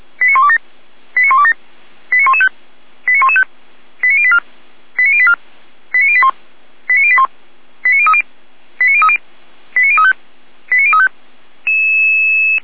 Five-tone selective call (ZVEI), heard as quick runs of five tones, tones of about 70 ms. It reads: 89127 89127 89115 89115 89084 89084 89081 89081 89030 89030 89035 89035